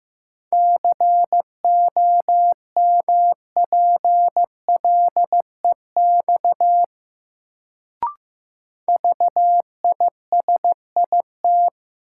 Morse code, heard as a keyed tone, about 15 wpm, 700 Hz. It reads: COMPLEX VISIT
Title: Sentences from Top 900 Words - New Word Drill - 15wpm